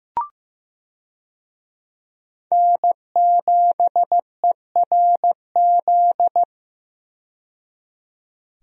Morse code, heard as a keyed tone, about 15 wpm, 700 Hz. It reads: N7ERZ
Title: USA Call Signs - 5 Characters - 15wpm